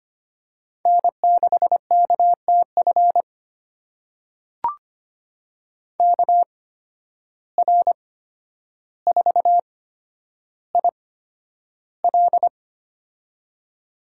Morse code, heard as a keyed tone, about 25 wpm, spaced as for 11 wpm, 700 Hz. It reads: N6KTF KR4IL
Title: USA Call Signs - 8x Character Spacing - 25wpm